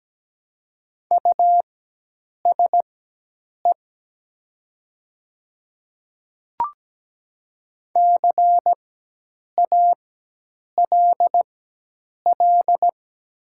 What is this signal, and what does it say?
Morse code, heard as a keyed tone, about 17 wpm, spaced as for 8 wpm, 700 Hz. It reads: USE CALL
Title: 4x Character Spacing - Top 100 Words - 17wpm